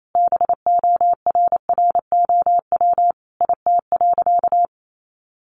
Morse code, heard as a keyed tone, about 28 wpm, 700 Hz. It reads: BORROW IT.
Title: Sentences from Top 1500 Words 28wpm